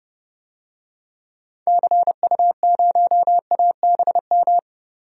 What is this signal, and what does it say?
Morse code, heard as a keyed tone, about 30 wpm, 700 Hz. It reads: CU0ABM
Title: DXCC-100 Call Signs - 6 Characters - 30wpm